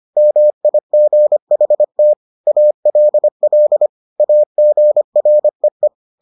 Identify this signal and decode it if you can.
Morse code, heard as a keyed tone, about 25 wpm, 600 Hz. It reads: MIGHT ALL AGREE